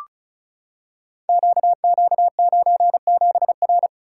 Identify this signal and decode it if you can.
Morse code, heard as a keyed tone, about 36 wpm, 700 Hz. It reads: QQ97R
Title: Code Groups - 5 Letter:Numbers - 35wpm